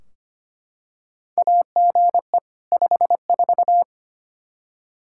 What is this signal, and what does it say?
Morse code, heard as a keyed tone, about 25 wpm, 700 Hz. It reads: AGE 54
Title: Practice QSO - Exchange 3 - More Info - 25wpm